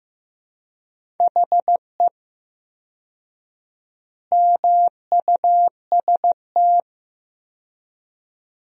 Morse code, heard as a keyed tone, about 15 wpm, 700 Hz. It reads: HE MUST
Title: 4x Word-Spacing - Sentences from Top 500 Words - 15wpm